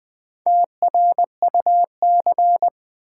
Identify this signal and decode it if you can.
Morse code, heard as a keyed tone, about 20 wpm, 700 Hz. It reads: TRUC
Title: Build A Word - 1 Letter At A Time - 5-Letter Words - 20wpm